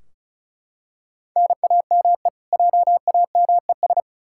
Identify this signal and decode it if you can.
Morse code, heard as a keyed tone, about 36 wpm, 700 Hz. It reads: NAME JAMES
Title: Practice QSO - Exchange 1 - Beginning - 35wpm